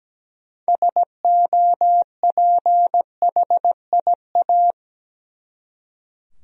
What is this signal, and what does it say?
Morse code, heard as a keyed tone, about 17 wpm, 700 Hz. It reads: SOPHIA
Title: USA Common Names - 17wpm